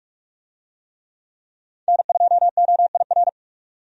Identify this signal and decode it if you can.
Morse code, heard as a keyed tone, about 46 wpm, 700 Hz. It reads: N2OIR
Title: USA Call Signs - 5 Characters - 45wpm